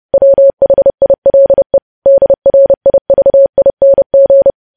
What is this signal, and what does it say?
Morse code, heard as a keyed tone, about 30 wpm, 550 Hz. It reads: WHILE DRIVING
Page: news headlines in morse code at 30 wpm